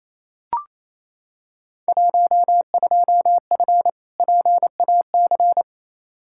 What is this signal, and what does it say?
Morse code, heard as a keyed tone, about 28 wpm, 700 Hz. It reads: 12F PAC
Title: Field Day Exchanges Encore - 28wpm